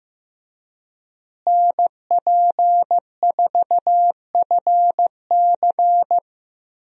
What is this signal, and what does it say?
Morse code, heard as a keyed tone, about 15 wpm, 700 Hz. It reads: NP4FC
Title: USA Call Signs - 5 Characters - Repeated 3x - 15wpm